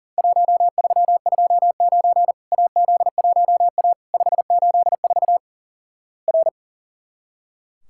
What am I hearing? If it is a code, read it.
Morse code, heard as a keyed tone, about 43 wpm, 700 Hz. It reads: 1329 AZ1A 584 R
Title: Worked All Europe DX Contest - QTC Exchange - 40wpm